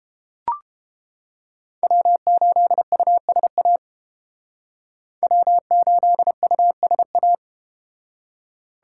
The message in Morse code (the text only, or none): W8USA W8USA